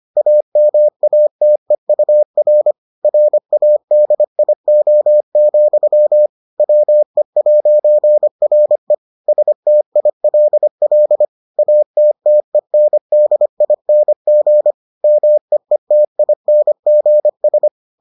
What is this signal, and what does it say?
Morse code, heard as a keyed tone, about 25 wpm, 600 Hz. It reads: AMATEUR RADIO, WE'RE STILL ATTENDING MEETINGS